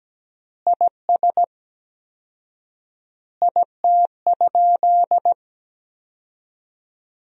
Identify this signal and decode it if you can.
Morse code, heard as a keyed tone, about 17 wpm, 700 Hz. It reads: IS IT?
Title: 4x Word-Spacing - Sentences from Top 300 Words - 17wpm